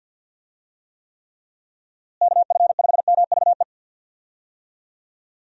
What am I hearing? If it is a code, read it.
Morse code, heard as a keyed tone, about 52 wpm, 700 Hz. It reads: KF5MVE
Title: USA Call Signs - 6 Characters - Repeated 3x - 50wpm